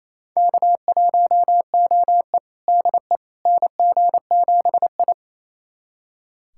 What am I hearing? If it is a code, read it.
Morse code, heard as a keyed tone, about 28 wpm, 700 Hz. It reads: K1OE DE NG7I
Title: USA Callsign DE Callsign - 4 Characters - 28wpm